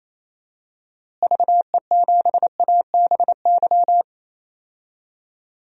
Morse code, heard as a keyed tone, about 28 wpm, 700 Hz. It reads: VE7ABY
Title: DXCC-100 Call Signs - 6 Characters - Repeated 2x - 28wpm